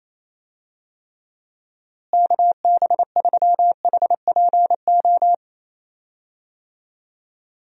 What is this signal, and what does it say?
Morse code, heard as a keyed tone, about 28 wpm, 700 Hz. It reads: KB3HPO
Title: USA Call Signs - 6 Characters - Repeated 3x - 28wpm